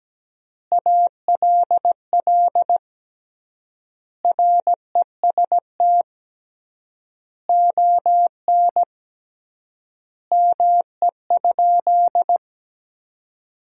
Morse code, heard as a keyed tone, about 17 wpm, 700 Hz. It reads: ALL REST ON ME?
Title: 3x Word-Spacing - Sentences from Top 500 Words - 17wpm